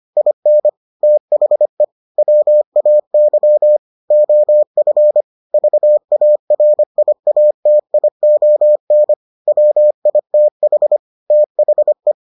Morse code, heard as a keyed tone, about 25 wpm, 600 Hz. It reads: IN THE WAY OF VARIATION WITH THE